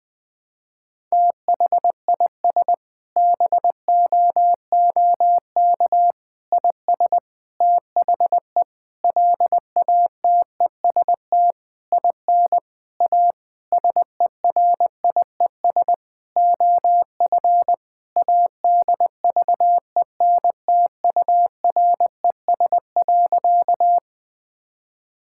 Morse code, heard as a keyed tone, about 20 wpm, 700 Hz. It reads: THIS BOOK IS THE LATEST IN A SERIES OF ADVENTURES.
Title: Sentences with 7-Word Phrases - 20wpm